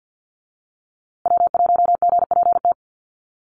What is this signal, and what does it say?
Morse code, heard as a keyed tone, about 50 wpm, 700 Hz. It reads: W1GPT
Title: USA Call Signs Repeated 3x 50wpm